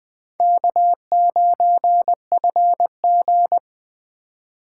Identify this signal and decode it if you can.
Morse code, heard as a keyed tone, about 20 wpm, 700 Hz. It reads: K9FG